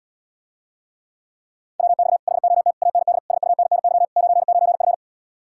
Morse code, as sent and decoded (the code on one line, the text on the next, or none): .-- -..- / .... --- - / .- -. -.. / ... ..- -. -. -.-- / ----. ----. ..-.
WX HOT AND SUNNY 99F